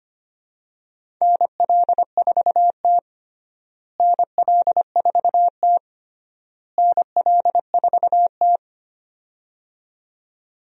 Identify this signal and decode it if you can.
Morse code, heard as a keyed tone, about 25 wpm, 700 Hz. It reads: NL4T NL4T NL4T
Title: USA Call Signs - 4 Characters - Repeated 3x - 25wpm